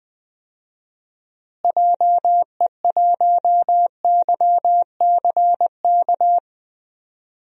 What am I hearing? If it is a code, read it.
Morse code, heard as a keyed tone, about 20 wpm, 700 Hz. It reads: JE1YCK
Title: DXCC-100 Call Signs - 6 Characters - Repeated 3x - 20wpm